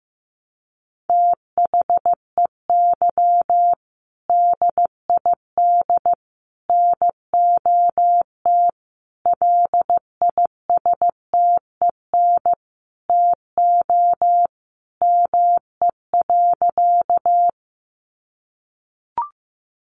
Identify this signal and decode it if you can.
Morse code, heard as a keyed tone, about 15 wpm, 700 Hz. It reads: THEY DID NOT LISTEN TO ME.
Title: Sentences from Top 600 Words 15wpm